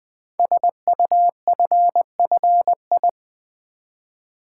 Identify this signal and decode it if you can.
Morse code, heard as a keyed tone, about 20 wpm, 700 Hz. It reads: SUFFI
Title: Build A Word - 5 Letters At A Time - 12-Letter Words - 20wpm